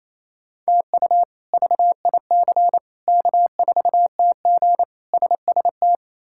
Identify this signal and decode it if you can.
Morse code, heard as a keyed tone, about 27 wpm, 700 Hz. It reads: TU VIC K4TG SST
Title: K1USN SST QSO Practice - 28wpm